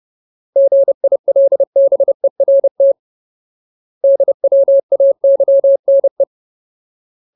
Morse code, heard as a keyed tone, about 30 wpm, 550 Hz. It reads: GILBERT DWAYNE
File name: Ultimate Word List / 30WPM Ultimate